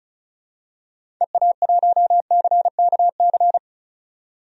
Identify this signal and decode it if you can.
Morse code, heard as a keyed tone, about 36 wpm, 700 Hz. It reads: EA1CKC
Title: DXCC-100 Call Signs - 6 Characters - 35wpm